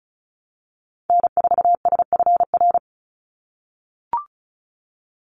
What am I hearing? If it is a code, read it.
Morse code, heard as a keyed tone, about 34 wpm, 700 Hz. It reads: N4SFR
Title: NA SOTA Activator Callsigns 35wpm